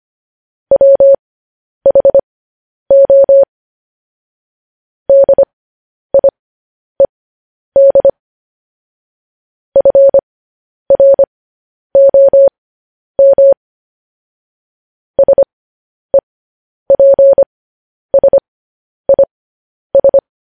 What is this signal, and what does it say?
Morse code, heard as a keyed tone, about 25 wpm, spaced as for 10 wpm, 550 Hz. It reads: WHO DIED FROM SEPSIS
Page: news-headlines-in-morse-code-at-10-wpm